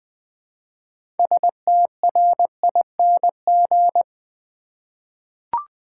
Morse code, heard as a keyed tone, about 20 wpm, 700 Hz. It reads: STRING
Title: Six Letter Words - 20wpm